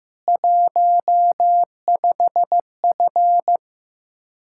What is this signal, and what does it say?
Morse code, heard as a keyed tone, about 16 wpm, 700 Hz. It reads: 15F
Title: Practice QSO - Exchange 2 - Info - 15wpm